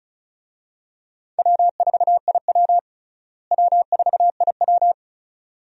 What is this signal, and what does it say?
Morse code, heard as a keyed tone, about 36 wpm, 700 Hz. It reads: W4IW W4IW
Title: USA Call Signs - 4 Characters - Repeated 2x - 35wpm